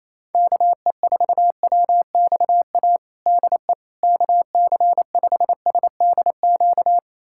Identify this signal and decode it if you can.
Morse code, heard as a keyed tone, about 28 wpm, 700 Hz. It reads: KE4WXA DE KC5SDQ
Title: USA Callsign DE Callsign - 6 Characters - 28wpm